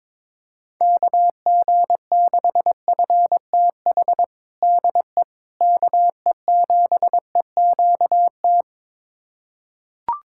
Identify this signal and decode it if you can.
Morse code, heard as a keyed tone, about 22 wpm, 700 Hz. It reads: KG6FTH DE KE7EQT
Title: USA Callsign DE Callsign - 6 Characters - 22wpm